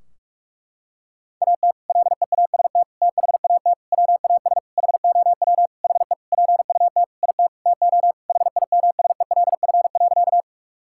Morse code, heard as a keyed tone, about 42 wpm, 700 Hz. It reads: AT LEAST THAT WAS HOW HE PUT IT TO HIMSELF.
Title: Sentences from Top 1600 Words 45wpm